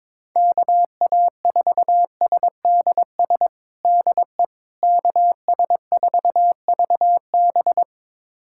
Morse code, heard as a keyed tone, about 22 wpm, 700 Hz. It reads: KA4SDS DE KS4VB
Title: USA Callsign DE Callsign 22wpm